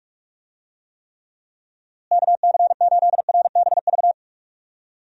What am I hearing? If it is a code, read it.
Morse code, heard as a keyed tone, about 46 wpm, 700 Hz. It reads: KC8RBV
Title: USA Call Signs - 6 Characters - Repeated 2x - 45wpm